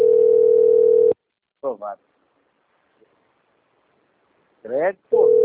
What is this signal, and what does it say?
Ringback tone (North America), 440 + 480 Hz, with speech.
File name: out- - - - - .5127